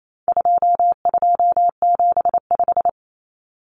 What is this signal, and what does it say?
Morse code, heard as a keyed tone, about 28 wpm, 700 Hz. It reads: 2275
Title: Common Cut Numbers - 4-Digit 28wpm